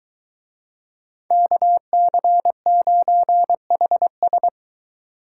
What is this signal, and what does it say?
Morse code, heard as a keyed tone, about 21 wpm, 700 Hz. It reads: KC9HS